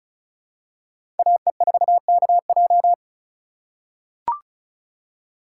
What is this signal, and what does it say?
Morse code, heard as a keyed tone, about 36 wpm, 700 Hz. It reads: AE4KJ
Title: USA Call Signs - 4x Character Spacing - 35wpm